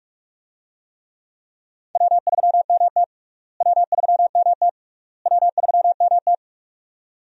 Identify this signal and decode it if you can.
Morse code, heard as a keyed tone, about 41 wpm, 700 Hz. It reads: W3MT W3MT W3MT
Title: USA Call Signs - 4 Characters - Repeated 3x - 45wpm